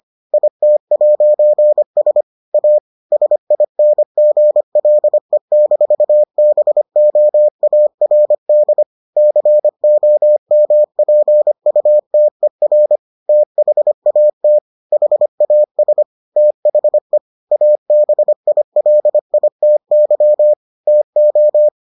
Morse code, heard as a keyed tone, about 25 wpm, 600 Hz. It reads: IT'S A SINGLE-BOARD COMPUTER THAT HAS THE ABILITY TO